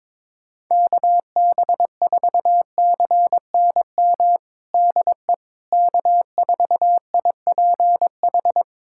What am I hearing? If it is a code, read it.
Morse code, heard as a keyed tone, about 21 wpm, 700 Hz. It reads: KB4CNM DE K4IPH
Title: USA Callsign DE Callsign 22wpm